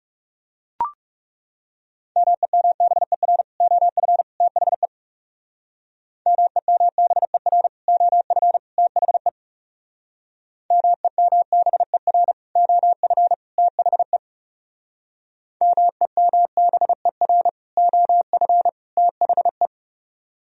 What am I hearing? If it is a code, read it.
Morse code, heard as a keyed tone, about 35 wpm, 700 Hz. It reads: MEMBER OF THE MEMBER OF THE MEMBER OF THE MEMBER OF THE
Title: Speed-Racing - Sets of 3 Words 30wpm